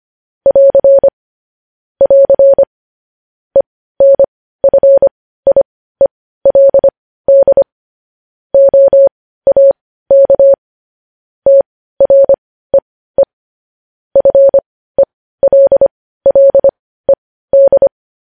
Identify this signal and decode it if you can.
Morse code, heard as a keyed tone, about 25 wpm, spaced as for 15 wpm, 550 Hz. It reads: <AR> <AR> ENFIELD OAK TREE FELLED
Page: news-headlines-in-morse-code-at-15-wpm